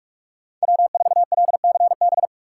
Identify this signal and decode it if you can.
Morse code, heard as a keyed tone, about 59 wpm, spaced as for 52 wpm, 700 Hz. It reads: W4LCB
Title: Speed Racing - USA 1x3 Call Signs 45wpm